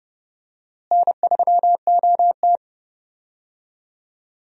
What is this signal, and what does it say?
Morse code, heard as a keyed tone, about 30 wpm, 700 Hz. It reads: N3OT
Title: USA Call Signs - 4 Characters - Repeated 3x - 30wpm